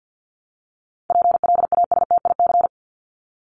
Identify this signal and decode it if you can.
Morse code, heard as a keyed tone, about 50 wpm, 700 Hz. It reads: PLASTIC
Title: Top 1001-1250 Words 50wpm